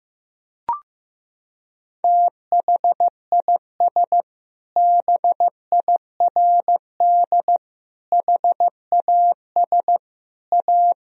Morse code, heard as a keyed tone, about 15 wpm, 700 Hz. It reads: THIS BIRD HAS A
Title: Sentences from Top 1000 Words - Review Part 2 - 15wpm